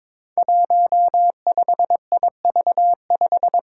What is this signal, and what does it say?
Morse code, heard as a keyed tone, about 21 wpm, 700 Hz. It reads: 15IV5
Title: Five Challenging Letter-Number Combinations 22wpm